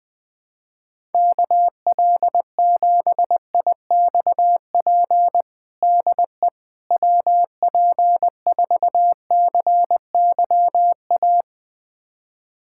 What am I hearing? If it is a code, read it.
Morse code, heard as a keyed tone, about 20 wpm, 700 Hz. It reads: KL7IXP DE WP4CYA
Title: USA Callsign DE Callsign - 6 Characters - 20wpm